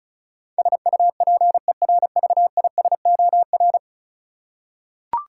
Morse code, heard as a keyed tone, about 36 wpm, 700 Hz. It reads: SUPERVISOR
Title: Practice QSO - Exchange 3 - More Info - 35wpm